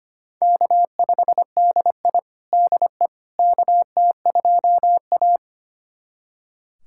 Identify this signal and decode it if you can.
Morse code, heard as a keyed tone, about 25 wpm, 700 Hz. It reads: K5DI DE KT2A
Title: USA Callsign DE Callsign - 4 Characters - 25wpm